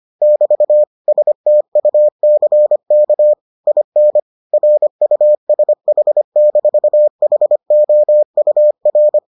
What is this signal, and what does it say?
Morse code, heard as a keyed tone, about 25 wpm, 600 Hz. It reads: <BT> STUCK IN RUSH-HOUR